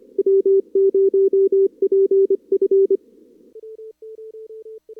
Morse code, heard as a keyed tone, about 25 wpm, 390 Hz. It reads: W0PF
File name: CWSprint / Feb2022 / k5tr-feb-na-sprint-cw-k5ot -0100